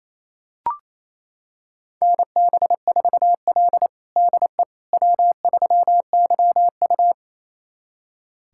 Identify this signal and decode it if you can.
Morse code, heard as a keyed tone, about 28 wpm, 700 Hz. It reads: NB4L DE W3YU
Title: USA Callsign DE Callsign - 4 Characters - 28wpm